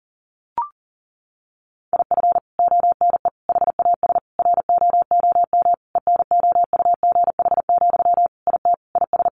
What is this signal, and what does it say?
Morse code, heard as a keyed tone, about 37 wpm, 700 Hz. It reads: IF ONE HAS ROOM ENOUGH, IT IS
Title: Sentences from Top 1400 Words 40wpm